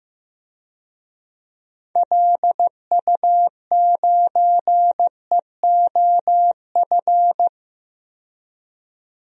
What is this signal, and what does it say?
Morse code, heard as a keyed tone, about 16 wpm, 700 Hz. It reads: LU9EOF
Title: DXCC-100 Call Signs - 6 Characters - 15wpm